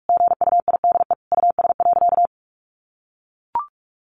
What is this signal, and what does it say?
Morse code, heard as a keyed tone, about 44 wpm, 700 Hz. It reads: GUIDE US.
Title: Sentences from Top 800 Words 45wpm